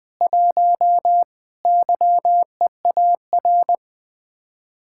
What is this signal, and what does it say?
Morse code, heard as a keyed tone, about 20 wpm, 700 Hz. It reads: 1 YEAR
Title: Speed-Racing Common QSO Phrases - 20wpm